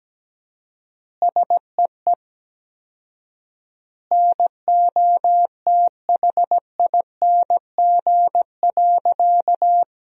Morse code, heard as a keyed tone, about 17 wpm, 700 Hz. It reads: SEE NOTHING.
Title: 4x Word-Spacing - Sentences from Top 500 Words - 17wpm